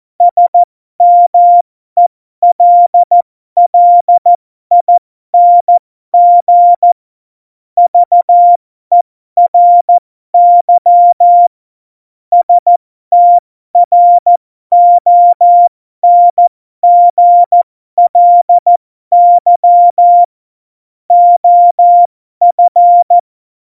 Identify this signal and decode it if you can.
Morse code, heard as a keyed tone, about 14 wpm, spaced as for 12 wpm, 700 Hz. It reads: SMELLING VERY STRONGLY OF
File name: Morsekurs / Witze